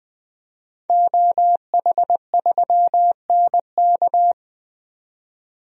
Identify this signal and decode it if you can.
Morse code, heard as a keyed tone, about 20 wpm, 700 Hz. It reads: OH3NK
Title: DXCC-100 Call Signs - 5 Characters - 20wpm